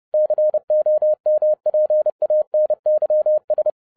Morse code, heard as a keyed tone, about 30 wpm, 610 Hz. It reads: COMPANYS